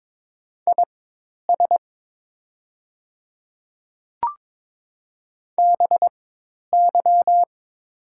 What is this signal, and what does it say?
Morse code, heard as a keyed tone, about 22 wpm, spaced as for 10 wpm, 700 Hz. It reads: IS BY
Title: 4x Character Spacing - Top 100 Words - 22wpm